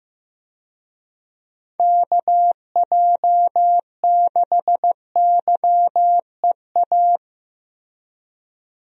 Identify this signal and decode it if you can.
Morse code, heard as a keyed tone, about 15 wpm, 700 Hz. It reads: KJ6YEA
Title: USA Call Signs - 6 Characters - Repeated 2x - 15wpm